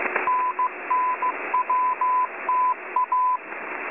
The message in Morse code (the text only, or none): NNWTA